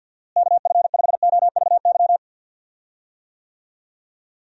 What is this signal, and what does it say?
Morse code, heard as a keyed tone, about 50 wpm, 700 Hz. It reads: KF5OVY
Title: USA Call Signs - 6 Characters - Repeated 2x - 50wpm